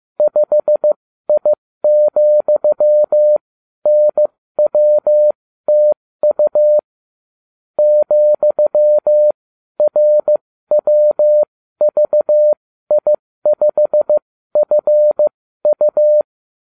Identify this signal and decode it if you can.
Morse code, heard as a keyed tone, about 15 wpm, spaced as for 12 wpm, 600 Hz. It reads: 5I,NWTU ,RWVI5FU